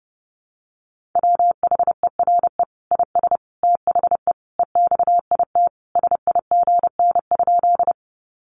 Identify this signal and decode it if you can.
Morse code, heard as a keyed tone, about 30 wpm, 700 Hz. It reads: WHERE IS THE EXIT SIGN?
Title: Sentences from Top 1100 Words 30wpm